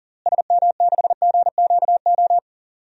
Morse code, heard as a keyed tone, about 39 wpm, 700 Hz. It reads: SM6GQO
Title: DXCC-100 Call Signs - 6 Characters - Repeated 2x - 40wpm